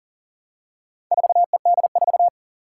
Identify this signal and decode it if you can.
Morse code, heard as a keyed tone, about 40 wpm, 700 Hz. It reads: 4ED4
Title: Four Challenging Letter-Number Combinations 40wpm